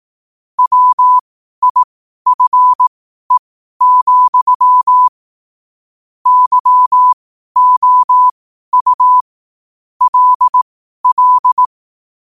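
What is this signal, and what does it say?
Morse code, heard as a keyed tone, about 18 wpm, spaced as for 12 wpm, 1000 Hz. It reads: WIFE, YOU LL
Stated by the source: Morse code sent at 12 words per minute.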